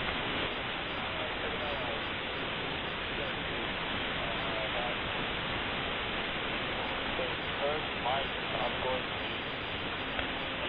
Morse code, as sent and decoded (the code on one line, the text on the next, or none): none